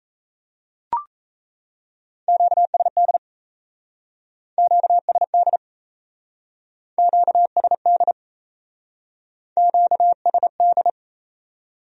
Morse code, heard as a keyed tone, about 35 wpm, 700 Hz. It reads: QSD QSD QSD QSD